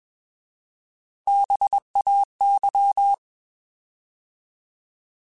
Morse code, heard as a keyed tone, about 21 wpm, 800 Hz. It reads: BAY
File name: Alphabetical / Alphabetical mp3s / Bs @20